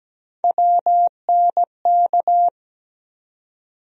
Morse code, heard as a keyed tone, about 17 wpm, 700 Hz. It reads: WNK
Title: USA Call Sign Postfixes - 3 Letters 17wpm